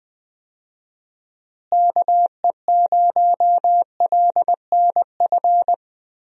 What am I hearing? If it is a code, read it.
Morse code, heard as a keyed tone, about 20 wpm, 700 Hz. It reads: KE0LNF
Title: USA Call Signs - 6 Characters - Repeated 3x - 20wpm